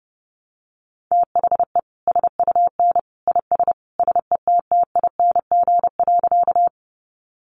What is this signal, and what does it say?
Morse code, heard as a keyed tone, about 30 wpm, 700 Hz. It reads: THE SUN IS SETTING.
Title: Sentences from Top 1300 Words 30wpm